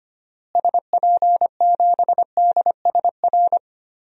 Morse code, heard as a keyed tone, about 25 wpm, 700 Hz. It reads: SP7DSR
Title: DXCC-100 Call Signs - 6 Characters - Repeated 3x - 25wpm